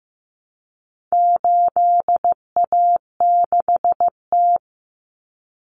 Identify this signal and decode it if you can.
Morse code, heard as a keyed tone, about 15 wpm, 700 Hz. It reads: 8A6T
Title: Common Cut Numbers - 4-Digit 15wpm